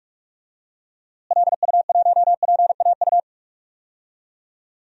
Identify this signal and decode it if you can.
Morse code, heard as a keyed tone, about 46 wpm, 700 Hz. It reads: LU1PAU
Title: DXCC-100 Call Signs - 6 Characters - Repeated 2x - 45wpm